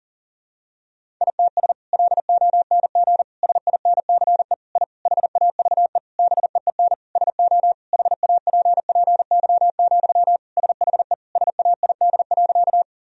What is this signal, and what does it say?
Morse code, heard as a keyed tone, about 40 wpm, 700 Hz. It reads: ITS LONG SINCE I HAVE BEEN SO HAPPY, SHE SAID.
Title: Sentences from Top 1000 Words - Review Part 1 - 40wpm